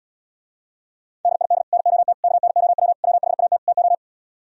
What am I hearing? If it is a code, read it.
Morse code, heard as a keyed tone, about 77 wpm, 700 Hz. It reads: DID NOT KNOW WHAT TO